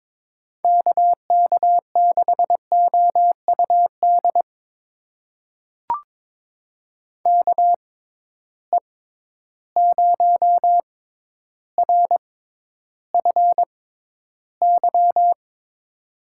Morse code, heard as a keyed tone, about 21 wpm, spaced as for 11 wpm, 700 Hz. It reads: KK6OUD KE0RFY
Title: USA Call Signs - 6 Characters - 6x Character Spacing - 22wpm